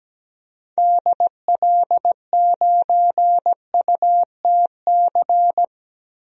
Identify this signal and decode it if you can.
Morse code, heard as a keyed tone, about 17 wpm, 700 Hz. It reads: DL9UTC
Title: DXCC-100 Call Signs - 6 Characters - Repeated 3x - 17wpm